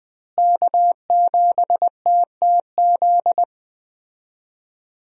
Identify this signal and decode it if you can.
Morse code, heard as a keyed tone, about 21 wpm, 700 Hz. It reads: K7TTZ
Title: USA Call Signs - 5 Characters - Repeated 3x - 20wpm